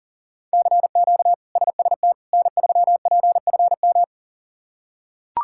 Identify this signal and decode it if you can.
Morse code, heard as a keyed tone, about 37 wpm, 700 Hz. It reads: CQ SST N3PFM
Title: K1USN SST QSO Practice - 40wpm